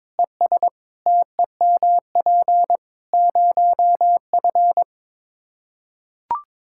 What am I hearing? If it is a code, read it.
Morse code, heard as a keyed tone, about 22 wpm, 700 Hz. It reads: ES TEMP 0F
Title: Practice QSO - Exchange 2 - Info - 22wpm